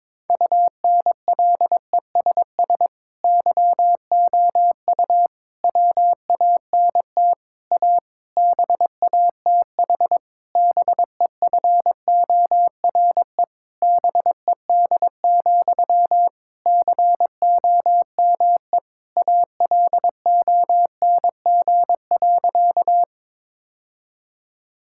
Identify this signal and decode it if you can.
Morse code, heard as a keyed tone, about 22 wpm, 700 Hz. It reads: UNLESS YOU WANT A BATH BEFORE BED, COME ALONG.
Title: Sentences from Top 1600 Words 22wpm